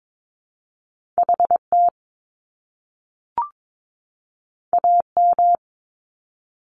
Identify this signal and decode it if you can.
Morse code, heard as a keyed tone, about 22 wpm, 700 Hz. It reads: HT AM
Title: Top Two Letter Combinations in Words 22wpm